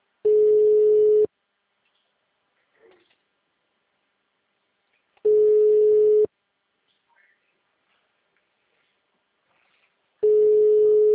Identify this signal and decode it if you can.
Ringback tone (Europe, 425 Hz).
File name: out- -2004- - - .5